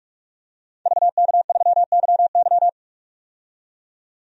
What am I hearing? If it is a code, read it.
Morse code, heard as a keyed tone, about 41 wpm, 700 Hz. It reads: VK3YY
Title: DXCC-100 Call Signs - 5 Characters - 45wpm